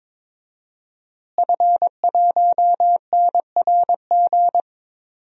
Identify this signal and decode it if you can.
Morse code, heard as a keyed tone, about 22 wpm, 700 Hz. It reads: F1NRG
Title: DXCC-100 Call Signs - 5 Characters - Repeated 3x - 22wpm